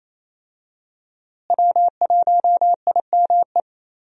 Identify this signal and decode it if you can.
Morse code, heard as a keyed tone, about 28 wpm, 700 Hz. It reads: W1IME
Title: USA Call Signs - 5 Characters - Repeated 2x - 28wpm